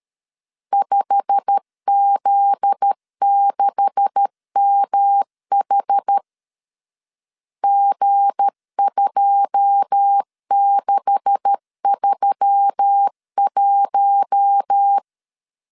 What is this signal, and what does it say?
Morse code, heard as a keyed tone, about 13 wpm, 800 Hz. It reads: 5Z6MH G2631
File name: CW / DET / 1 9 1500C S C 12wpm 3811